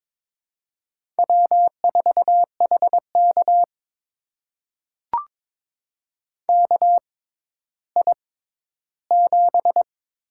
Morse code, heard as a keyed tone, about 22 wpm, spaced as for 13 wpm, 700 Hz. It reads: W4HK KI7
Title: USA Call Signs - 6x Character Spacing - 22wpm